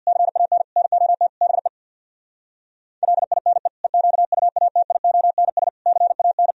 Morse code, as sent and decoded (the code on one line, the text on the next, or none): -.-. .- -. / -. --- - / -... / .-.. .. -. / . --.- ..- .- - .. --- -. ... / -.-. .- -.
CAN NOT BE LINE EQUATIONS CAN